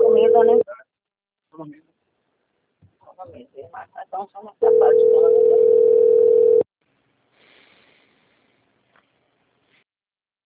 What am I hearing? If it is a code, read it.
Ringback tone (North America), 440 + 480 Hz, with speech.
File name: rg-1600-unknown- - - .23451